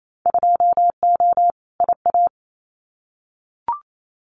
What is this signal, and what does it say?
Morse code, heard as a keyed tone, about 28 wpm, 700 Hz. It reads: 2O IA
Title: Winter Field Day Exchanges 28wpm